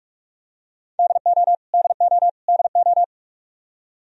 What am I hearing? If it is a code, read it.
Morse code, heard as a keyed tone, about 46 wpm, 700 Hz. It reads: DO DO DO
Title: Top 100 Words Repeated 45wpm